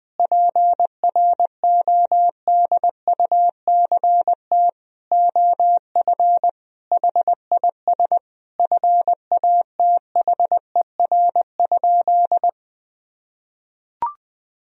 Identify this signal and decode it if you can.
Morse code, heard as a keyed tone, about 20 wpm, 700 Hz. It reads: PRODUCT OF HIS FATHER?
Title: Sentences from Top 1000 Words - Review Part 2 - 20wpm